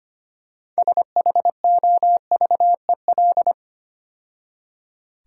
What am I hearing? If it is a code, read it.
Morse code, heard as a keyed tone, about 30 wpm, 700 Hz. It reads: SHOVEL
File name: Easy Words - Speed-Racing with Hints - 25wpm